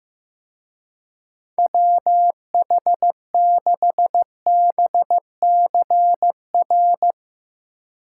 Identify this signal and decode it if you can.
Morse code, heard as a keyed tone, about 18 wpm, 700 Hz. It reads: WH6BCR
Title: Speed Racing - USA 2x3 Call Signs 15wpm